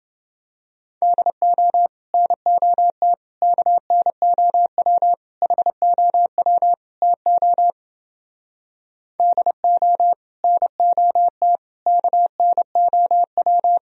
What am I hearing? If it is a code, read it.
Morse code, heard as a keyed tone, about 29 wpm, 700 Hz. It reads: DO NOT KNOW HOW TO DO NOT KNOW